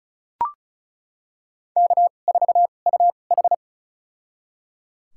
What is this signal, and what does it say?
Morse code, heard as a keyed tone, about 36 wpm, spaced as for 25 wpm, 700 Hz. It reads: K4UH
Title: USA Call Signs - 2x Character Spacing - 35wpm